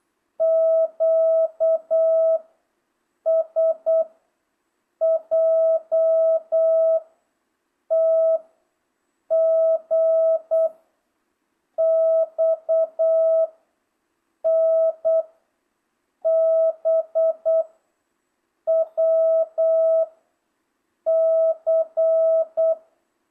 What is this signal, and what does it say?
Morse code, heard as a keyed tone, about 8 wpm, 650 Hz. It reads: Q S J T G X N B W C